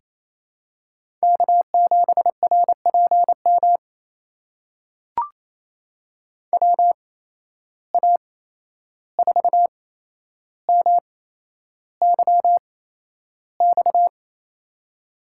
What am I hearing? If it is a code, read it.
Morse code, heard as a keyed tone, about 28 wpm, spaced as for 11 wpm, 700 Hz. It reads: K7RPM WA4MYX
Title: USA Call Signs - 8x Character Spacing - 28wpm